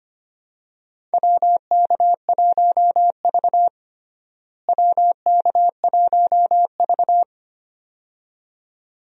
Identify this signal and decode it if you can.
Morse code, heard as a keyed tone, about 25 wpm, 700 Hz. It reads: WK1V WK1V
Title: USA Call Signs - 4 Characters - Repeated 2x - 25wpm